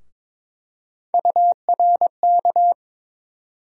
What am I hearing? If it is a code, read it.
Morse code, heard as a keyed tone, about 21 wpm, 700 Hz. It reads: URK
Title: USA Call Sign Postfixes - 3 Letters 22wpm